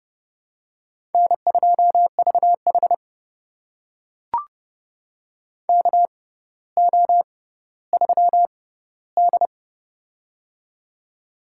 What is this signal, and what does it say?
Morse code, heard as a keyed tone, about 30 wpm, spaced as for 15 wpm, 700 Hz. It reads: N2VH KO3D
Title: USA Call Signs - 4 Characters - 6x Character Spacing - 30wpm